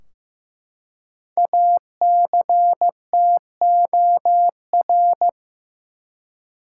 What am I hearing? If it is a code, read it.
Morse code, heard as a keyed tone, about 15 wpm, 700 Hz. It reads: ACTOR
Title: Five Letter Words - 15wpm